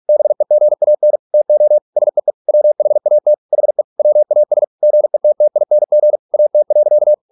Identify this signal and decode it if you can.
Morse code, heard as a keyed tone, about 46 wpm, 600 Hz. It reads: BEGAN TO SEE WHAT HE WAS GETTING AT.